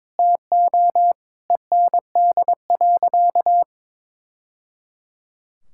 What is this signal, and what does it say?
Morse code, heard as a keyed tone, about 22 wpm, 700 Hz. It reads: TO END.
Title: Sentences from Top 1000 Words - Review Part 5 - 22wpm